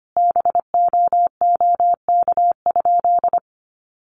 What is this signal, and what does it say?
Morse code, heard as a keyed tone, about 25 wpm, 700 Hz. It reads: BOOK?
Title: Sentences from Top 1200 Words 25wpm